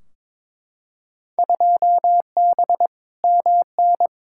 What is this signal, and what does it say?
Morse code, heard as a keyed tone, about 22 wpm, 700 Hz. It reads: 2B MN
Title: Field Day Exchanges - 22wpm